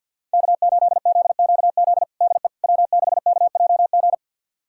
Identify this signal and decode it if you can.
Morse code, heard as a keyed tone, about 51 wpm, 700 Hz. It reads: K8ZQZ DE W6KJG
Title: USA Callsign DE Callsign - 5 Characters - 50wpm